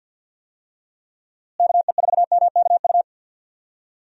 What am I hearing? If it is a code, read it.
Morse code, heard as a keyed tone, about 50 wpm, 700 Hz. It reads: KE4MKU
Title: USA Call Signs - 6 Characters - Repeated 3x - 50wpm